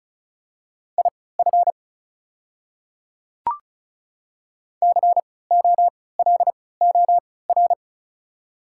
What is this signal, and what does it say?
Morse code, heard as a keyed tone, about 34 wpm, spaced as for 20 wpm, 700 Hz. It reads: IF COLOR
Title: 3x Character Spacing - Top 500 Words - 35wpm